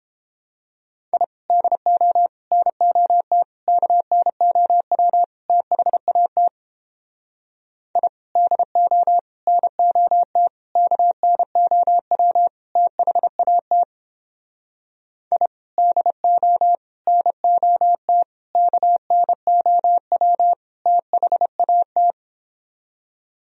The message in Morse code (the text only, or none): I DO NOT KNOW THAT I DO NOT KNOW THAT I DO NOT KNOW THAT